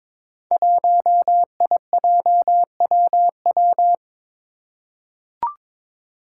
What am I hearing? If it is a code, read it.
Morse code, heard as a keyed tone, about 22 wpm, 700 Hz. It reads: 1IJWW
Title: Five Challenging Letter-Number Combinations 22wpm